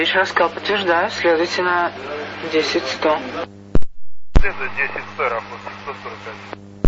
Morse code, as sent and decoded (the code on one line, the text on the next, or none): none